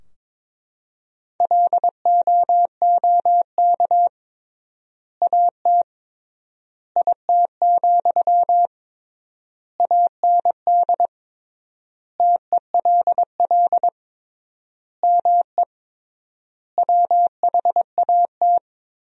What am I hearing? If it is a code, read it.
Morse code, heard as a keyed tone, about 22 wpm, 700 Hz. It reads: LOOK AT IT, AND TELL ME WHAT
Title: 3x Word-Spacing - Sentences from Top 200 Words - 22wpm